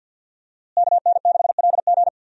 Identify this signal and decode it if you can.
Morse code, heard as a keyed tone, about 52 wpm, 700 Hz. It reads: KN6LG
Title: USA Call Signs - 5 Characters - Repeated 2x - 50wpm